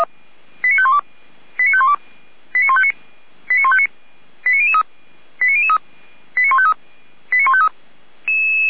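Five-tone selective call (ZVEI), heard as quick runs of five tones, tones of about 70 ms. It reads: 89512 89512 89159 89159 89003 89003 89153 89153
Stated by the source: ZVEI five-tone selective call received on an SDR.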